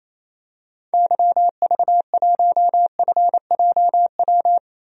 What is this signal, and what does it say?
Morse code, heard as a keyed tone, about 28 wpm, 700 Hz. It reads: YV1FJW
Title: DXCC-100 Call Signs - 6 Characters - 28wpm